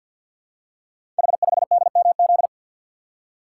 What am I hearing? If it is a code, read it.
Morse code, heard as a keyed tone, about 48 wpm, 700 Hz. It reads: H5DMZ
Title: Code Groups - 5 Letter:Numbers - 50wpm